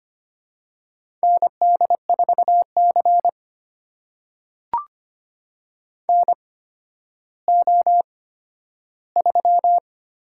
Morse code, heard as a keyed tone, about 25 wpm, spaced as for 12 wpm, 700 Hz. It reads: ND4C NO3
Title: USA Call Signs - 4 Characters - 8x Character Spacing - 25wpm